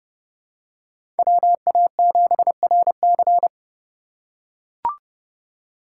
Morse code, heard as a keyed tone, about 30 wpm, 700 Hz. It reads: WA7RC
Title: USA Call Signs - 5 Characters - 2x Character Spacing - 30wpm